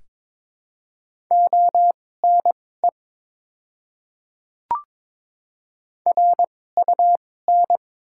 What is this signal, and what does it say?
Morse code, heard as a keyed tone, about 22 wpm, spaced as for 16 wpm, 700 Hz. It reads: ONE RUN
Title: 2x Character Spacing - Top 500 Words - 22wpm